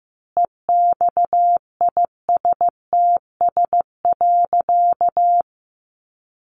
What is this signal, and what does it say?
Morse code, heard as a keyed tone, about 15 wpm, 700 Hz. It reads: EXISTS.
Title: A Sound Of Thunder 15wpm